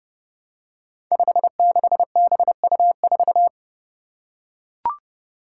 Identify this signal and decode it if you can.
Morse code, heard as a keyed tone, about 30 wpm, 700 Hz. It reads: 56BU4
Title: Five Challenging Letter-Number Combinations 30wpm